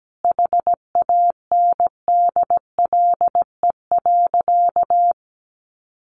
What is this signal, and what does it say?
Morse code, heard as a keyed tone, about 17 wpm, 700 Hz. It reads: HANDLE.
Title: Sentences from Top 1100 Words 17wpm